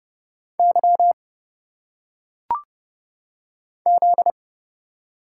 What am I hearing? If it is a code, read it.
Morse code, heard as a keyed tone, about 29 wpm, 700 Hz. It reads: Y Z
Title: Single Letters - Phonetic Pronunciation 30wpm